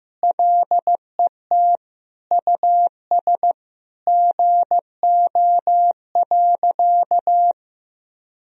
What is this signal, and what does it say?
Morse code, heard as a keyed tone, about 15 wpm, 700 Hz. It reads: LET US GO.
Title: Sentences from Top 400 Words and Abbreviations 15wpm